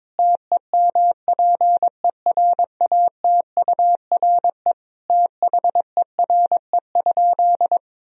Morse code, heard as a keyed tone, about 22 wpm, 700 Hz. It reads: TEMPERATURE THERE?
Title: Sentences from Top 1000 Words - Review Part 4 - 22wpm